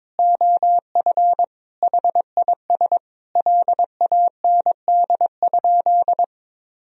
Morse code, heard as a keyed tone, about 22 wpm, 700 Hz. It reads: OF HIS LAND?
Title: Sentences from Top 1000 Words - Review Part 2 - 22wpm